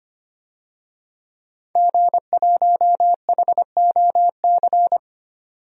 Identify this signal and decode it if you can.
Morse code, heard as a keyed tone, about 25 wpm, 690 Hz. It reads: G1HOC
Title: DXCC-100 Call Signs - 5 Characters - 25wpm